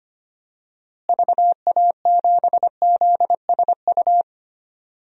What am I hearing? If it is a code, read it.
Morse code, heard as a keyed tone, about 25 wpm, 700 Hz. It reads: VA7ZSU
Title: DXCC-100 Call Signs - 6 Characters - Repeated 2x - 25wpm